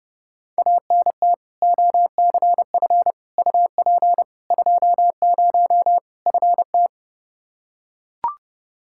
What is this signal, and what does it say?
Morse code, heard as a keyed tone, about 29 wpm, 700 Hz. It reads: ANT OCF UP 20 FT
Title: K1USN SST QSO Practice 30wpm